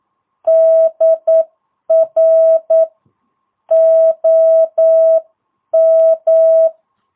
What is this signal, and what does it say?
Morse code, heard as a keyed tone, about 9 wpm, 650 Hz. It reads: DR OM
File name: jh8cbh / 2022cw